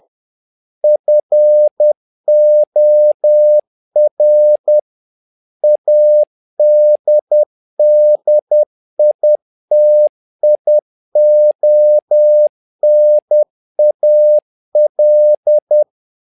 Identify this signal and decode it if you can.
Morse code, heard as a keyed tone, about 10 wpm, 600 Hz. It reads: FOR ADDITIONAL
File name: learning center / tower trouble / tower trouble 10wpm